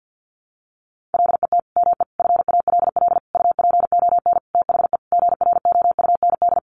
Morse code, heard as a keyed tone, about 50 wpm, 700 Hz. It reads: LET ME FALL UPON THE GROUND
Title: Sentences from Top 1100 Words 50wpm